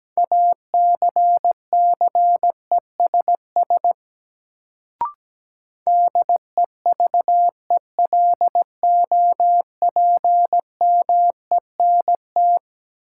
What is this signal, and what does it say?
Morse code, heard as a keyed tone, about 17 wpm, 700 Hz. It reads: ACCESS DEVELOPMENT
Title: Sentences from Top 1000 Words - Review Part 4 - New Word Drill - 17wpm